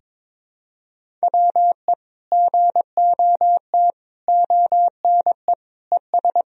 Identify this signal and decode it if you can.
Morse code, heard as a keyed tone, about 22 wpm, 700 Hz. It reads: WE GOT ONE ES
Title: Sentences from Top 300 Words and Abbreviations 22wpm